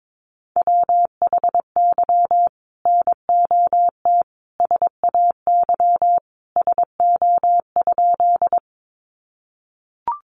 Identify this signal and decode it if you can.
Morse code, heard as a keyed tone, about 22 wpm, 700 Hz. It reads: WHY NOT SAY SO?
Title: Sentences from Top 600 Words 22wpm